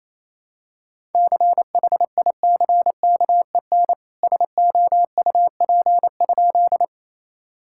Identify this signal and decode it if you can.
Morse code, heard as a keyed tone, about 28 wpm, 700 Hz. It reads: CHICKEN SOUP?
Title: Sentences from Top 1000 Words - Review Part 6 - 28wpm